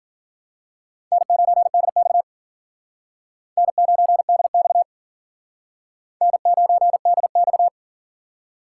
Morse code, heard as a keyed tone, about 47 wpm, 700 Hz. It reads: N9DX N9DX N9DX
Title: Speed Racing - USA 1x2 Call Signs 40wpm